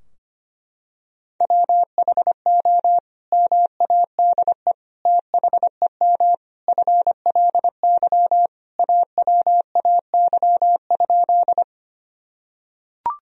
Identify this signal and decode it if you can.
Morse code, heard as a keyed tone, about 25 wpm, 700 Hz. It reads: WHO MADE THEM FLY AWAY?
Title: Sentences from Top 1000 Words - Review Part 2 - 25wpm